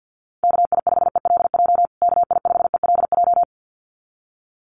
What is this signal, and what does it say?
Morse code, heard as a keyed tone, about 48 wpm, 700 Hz. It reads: KI5ELJ KI5ELJ
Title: USA Call Signs Repeated 2x 50wpm